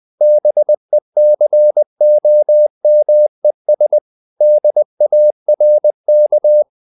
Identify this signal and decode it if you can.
Morse code, heard as a keyed tone, about 20 wpm, 600 Hz. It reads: BECOMES DARK
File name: the old man and the sea part 2 20wpm